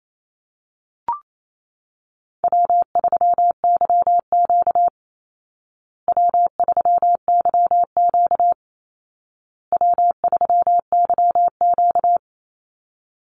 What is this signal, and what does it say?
Morse code, heard as a keyed tone, about 28 wpm, 700 Hz. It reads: W3YQ W3YQ W3YQ
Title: Call Signs Repeated 2x - 28wpm